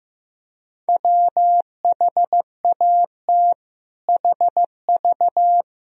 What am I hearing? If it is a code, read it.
Morse code, heard as a keyed tone, about 15 wpm, 700 Hz. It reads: WHAT HV
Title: Sentences from Top 400 Words and Abbreviations 15wpm